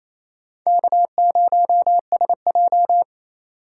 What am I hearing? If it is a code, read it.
Morse code, heard as a keyed tone, about 28 wpm, 700 Hz. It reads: K0SJ
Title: USA Call Signs - 4 Characters - 3x Character Spacing - 28wpm